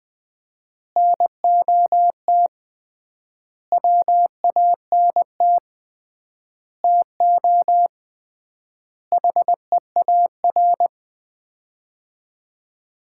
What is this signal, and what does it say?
Morse code, heard as a keyed tone, about 20 wpm, 700 Hz. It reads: NOT WANT TO HEAR
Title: 3x Word-Spacing - Sets of 5 Words - 20wpm